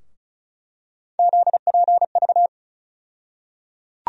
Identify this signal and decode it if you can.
Morse code, heard as a keyed tone, about 34 wpm, 700 Hz. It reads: ZPV
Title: USA Call Sign Postfixes - 3 Letters 35wpm